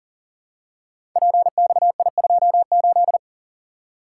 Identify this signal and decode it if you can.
Morse code, heard as a keyed tone, about 39 wpm, 700 Hz. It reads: PXI28
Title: Code Groups - 5 Letter:Numbers - 40wpm